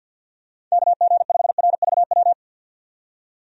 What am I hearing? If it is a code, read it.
Morse code, heard as a keyed tone, about 52 wpm, 700 Hz. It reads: KG5RVW
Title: USA Call Signs - 6 Characters - 50wpm